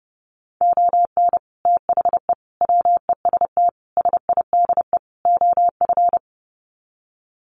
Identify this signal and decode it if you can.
Morse code, heard as a keyed tone, about 30 wpm, 700 Hz. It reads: ON THE WEST SIDE OF
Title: Sets of 5 Words Encore2 30wpm